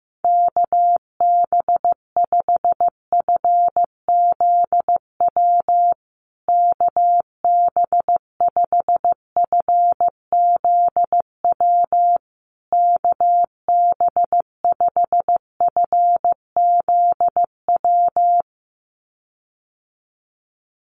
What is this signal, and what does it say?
Morse code, heard as a keyed tone, about 15 wpm, 700 Hz. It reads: KB5FZW KB5FZW KB5FZW
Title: USA Call Signs Repeated 3x 15wpm